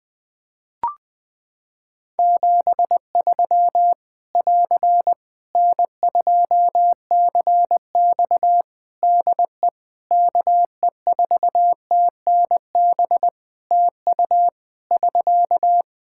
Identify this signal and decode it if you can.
Morse code, heard as a keyed tone, about 20 wpm, 700 Hz. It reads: 73 <AR> N2CX DE KE4TNB TU <SK>
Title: Practice QSO - Exchange 4 - Ending - 20wpm